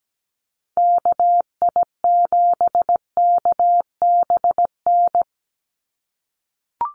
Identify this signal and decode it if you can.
Morse code, heard as a keyed tone, about 17 wpm, 700 Hz. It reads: KI7KBN
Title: USA Call Signs 17wpm